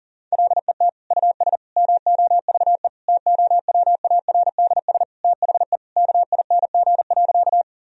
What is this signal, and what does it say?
Morse code, heard as a keyed tone, about 40 wpm, 700 Hz. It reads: LET US MOVE TOWARDS THE KING.
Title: Sentences from Top 1000 Words - Review Part 2 - 40wpm